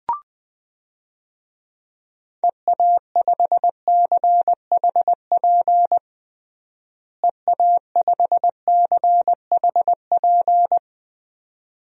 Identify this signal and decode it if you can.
Morse code, heard as a keyed tone, about 20 wpm, 700 Hz. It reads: EA5CHP EA5CHP
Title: DXCC-100 Call Signs - 6 Characters - Repeated 2x - 20wpm